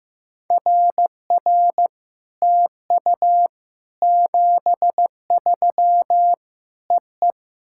Morse code, heard as a keyed tone, about 15 wpm, 690 Hz. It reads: RR TU 73 EE
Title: SOTA QSO Practice 15wpm